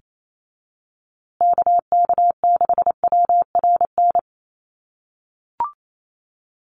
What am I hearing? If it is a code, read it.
Morse code, heard as a keyed tone, about 28 wpm, 700 Hz. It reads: KK6WRN
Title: USA Call Signs Repeated 2x 28wpm